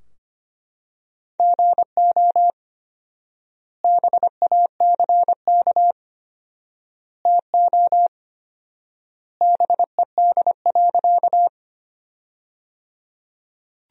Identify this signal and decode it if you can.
Morse code, heard as a keyed tone, about 25 wpm, 700 Hz. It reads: GO BACK TO BED.
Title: 4x Word-Spacing - Sentences from Top 500 Words - 25wpm